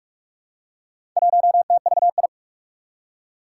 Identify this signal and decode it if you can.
Morse code, heard as a keyed tone, about 44 wpm, 700 Hz. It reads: 1TVI
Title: Four Challenging Letter-Number Combinations 45wpm